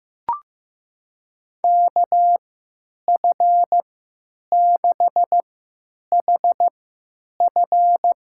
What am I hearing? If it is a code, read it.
Morse code, heard as a keyed tone, about 15 wpm, spaced as for 9 wpm, 700 Hz. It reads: KF6HF
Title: USA Call Signs - 3x Character Spacing - 15wpm